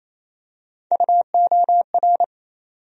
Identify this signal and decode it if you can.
Morse code, heard as a keyed tone, about 27 wpm, 700 Hz. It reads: UOR
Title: USA Call Sign Postfixes - 3 Letters 28wpm